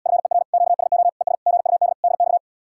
Morse code, heard as a keyed tone, about 80 wpm, 700 Hz. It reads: FER QSO ES GUD DX